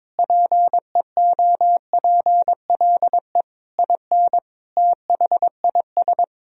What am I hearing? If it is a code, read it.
Morse code, heard as a keyed tone, about 22 wpm, 700 Hz. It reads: PEOPLE IN THIS
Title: Sentences from Top 1000 Words - Review Part 3 - 22wpm